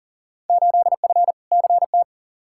Morse code, heard as a keyed tone, about 40 wpm, 700 Hz. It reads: 8F CT
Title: Field Day Exchanges Encore - 40wpm